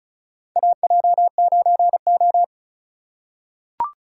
Morse code, heard as a keyed tone, about 36 wpm, 700 Hz. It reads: AJ9O
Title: USA Call Signs - 4 Characters - 3x Character Spacing - 35wpm